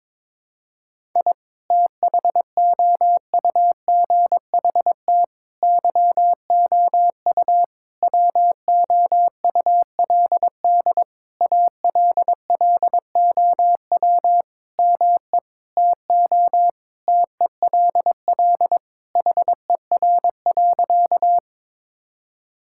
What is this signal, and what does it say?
Morse code, heard as a keyed tone, about 22 wpm, 700 Hz. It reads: I THOUGHT YOU WOULD ALLOW ME TO TELL HER.
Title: Sentences from Top 1000 Words - Review Part 5 - 22wpm